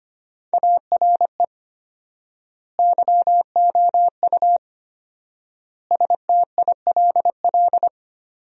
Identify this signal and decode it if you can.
Morse code, heard as a keyed tone, about 25 wpm, 700 Hz. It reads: ARE YOU STILL
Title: 4x Word-Spacing - Sentences from Top 400 Words - 25wpm